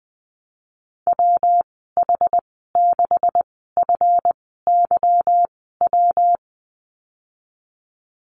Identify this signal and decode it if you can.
Morse code, heard as a keyed tone, about 20 wpm, spaced as for 15 wpm, 700 Hz. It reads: WH6FYW
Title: 154 Lesson 25g - Letter p - Call Signs 20wpm